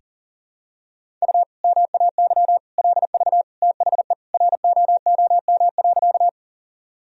Morse code, heard as a keyed tone, about 40 wpm, 700 Hz. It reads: U MAY LV THE ROOM.
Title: Sentences from Top 400 Words and Abbreviations 40wpm